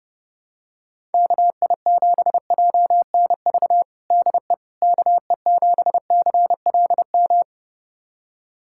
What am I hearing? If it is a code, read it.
Morse code, heard as a keyed tone, about 30 wpm, 700 Hz. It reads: KI7JNV DE KE7CLM
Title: USA Callsign DE Callsign - 6 Characters - 30wpm